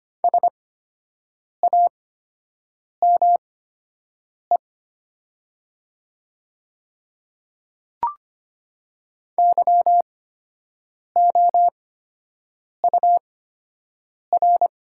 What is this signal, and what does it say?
Morse code, heard as a keyed tone, about 25 wpm, spaced as for 7 wpm, 700 Hz. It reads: SAME YOUR
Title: 8x Character Spacing - Top 200 Words - 25wpm